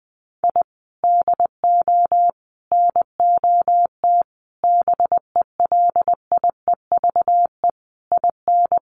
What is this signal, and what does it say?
Morse code, heard as a keyed tone, about 20 wpm, 700 Hz. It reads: I DO NOT BELIEVE IN
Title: Sets of 5 Words Encore 20wpm